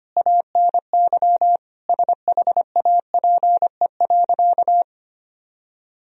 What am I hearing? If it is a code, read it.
Morse code, heard as a keyed tone, about 25 wpm, 700 Hz. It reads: ANY SHAPE.
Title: Sentences from Top 1000 Words - Review Part 3 - 25wpm